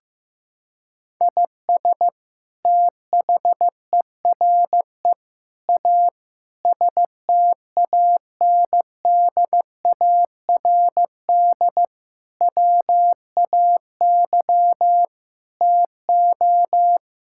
Morse code, heard as a keyed tone, about 15 wpm, 700 Hz. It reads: IS THERE A STANDARD WAY TO
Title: Sentences from Top 1000 Words - Review Part 6 - 15wpm